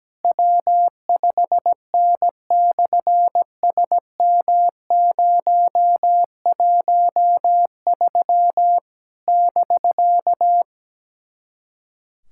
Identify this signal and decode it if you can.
Morse code, heard as a keyed tone, about 17 wpm, 700 Hz. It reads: W5N/SM013 <BK>
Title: SOTA QSO Practice 17wpm